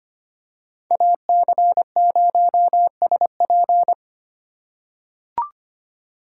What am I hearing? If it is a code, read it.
Morse code, heard as a keyed tone, about 25 wpm, 700 Hz. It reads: AC0SP
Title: USA Call Signs - 2x Character Spacing - 25wpm